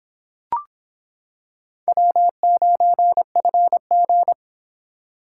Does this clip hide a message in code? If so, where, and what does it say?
Yes: Morse code from 1.9 to 4.3 s, heard as a keyed tone, about 26 wpm, 700 Hz. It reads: W9FG